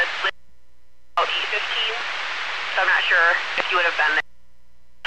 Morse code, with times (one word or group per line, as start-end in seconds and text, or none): none